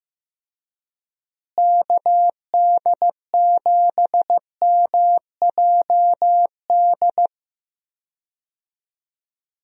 Morse code, heard as a keyed tone, about 15 wpm, 700 Hz. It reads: KD7MJD
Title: USA Call Signs - 6 Characters - Repeated 3x - 15wpm